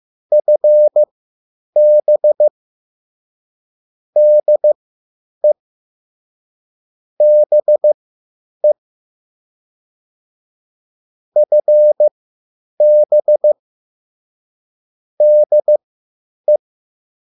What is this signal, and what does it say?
Morse code, heard as a keyed tone, about 15 wpm, spaced as for 9 wpm, 600 Hz. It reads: FB DE BE FB DE